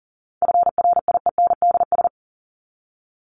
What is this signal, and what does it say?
Morse code, heard as a keyed tone, about 40 wpm, 700 Hz. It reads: FRIENDS
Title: ES:S Suffix - Top 100 Words with Base Words 40wpm